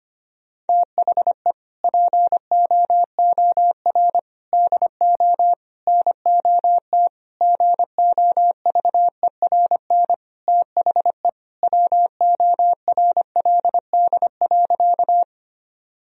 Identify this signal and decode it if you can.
Morse code, heard as a keyed tone, about 25 wpm, 700 Hz. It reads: THE POOR DO NOT GOVERN THE WORLD.
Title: Sentences from Top 1000 Words - Review Part 2 - 25wpm